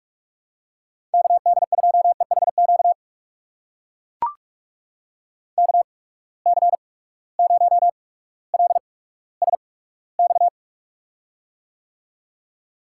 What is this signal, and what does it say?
Morse code, heard as a keyed tone, about 43 wpm, spaced as for 19 wpm, 700 Hz. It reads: KD2EHQ KC0LSX
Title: USA Call Signs - 6 Characters - 8x Character Spacing - 45wpm